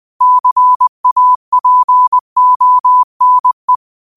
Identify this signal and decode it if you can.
Morse code, heard as a keyed tone, about 20 wpm, 1000 Hz. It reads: CAPONE